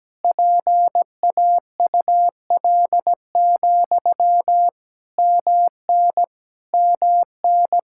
Morse code, heard as a keyed tone, about 17 wpm, 700 Hz. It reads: PAUL, MN MN
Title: Practice QSO - Exchange 1 - Beginning - 17wpm